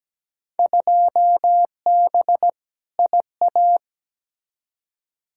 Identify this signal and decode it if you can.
Morse code, heard as a keyed tone, about 17 wpm, 700 Hz. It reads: 2B IA
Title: Field Day Exchanges Encore - 17wpm